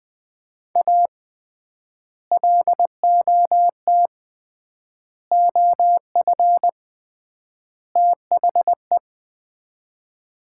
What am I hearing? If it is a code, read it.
Morse code, heard as a keyed tone, about 21 wpm, 700 Hz. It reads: A LOT OF THE
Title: 3x Word-Spacing - Sets of 4 Words - 20wpm